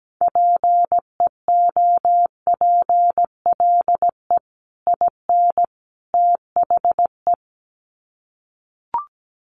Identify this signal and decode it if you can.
Morse code, heard as a keyed tone, about 17 wpm, 700 Hz. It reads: PEOPLE IN THE
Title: Sets of 3 Words 17wpm